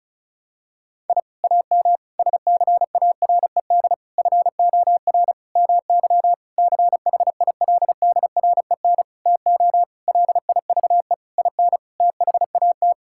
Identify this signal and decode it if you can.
Morse code, heard as a keyed tone, about 36 wpm, 700 Hz. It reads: I AM SCARED FOR MY CHILDREN TO LIVE IN THAT
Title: Sentences from Top 1000 Words - Review Part 5 - 35wpm